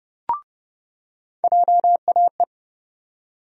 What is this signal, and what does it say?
Morse code, heard as a keyed tone, about 30 wpm, 700 Hz. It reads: JAE
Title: USA Call Sign Postfixes - 3 Letters 30wpm